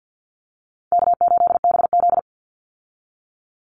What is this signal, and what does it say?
Morse code, heard as a keyed tone, about 50 wpm, 700 Hz. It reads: K8BZ
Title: Easy Call Signs Repeated 2x - 50wpm